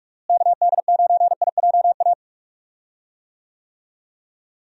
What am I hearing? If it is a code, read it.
Morse code, heard as a keyed tone, about 42 wpm, 700 Hz. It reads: KD9IJA
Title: USA Call Signs - 6 Characters - Repeated 3x - 45wpm